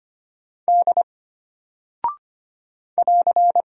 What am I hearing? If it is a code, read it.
Morse code, heard as a keyed tone, about 25 wpm, 700 Hz. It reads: D <AR>
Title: Single Character (Letter:Number:Punctuation:Prosign) Alt 25wpm